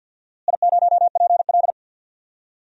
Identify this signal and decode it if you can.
Morse code, heard as a keyed tone, about 46 wpm, 700 Hz. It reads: I0PL
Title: DXCC-100 Call Signs - 4 Characters - Repeated 2x - 50wpm